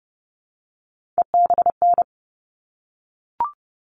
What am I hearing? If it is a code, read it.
Morse code, heard as a keyed tone, about 30 wpm, 700 Hz. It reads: EBN
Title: All Cut Numbers - 3-Digit 30wpm